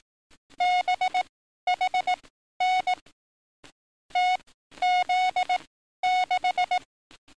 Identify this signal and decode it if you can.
Morse code, heard as a keyed tone, about 18 wpm, spaced as for 12 wpm, 730 Hz. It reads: BHN TZ6
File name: training / NAVPERS 11049to11060 / NAVPERS 11052 RA RB